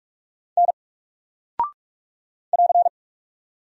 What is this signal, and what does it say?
Morse code, heard as a keyed tone, about 44 wpm, 700 Hz. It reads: N <AR>
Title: Single Character (Letter:Number:Punctuation:Prosign) Alt 45wpm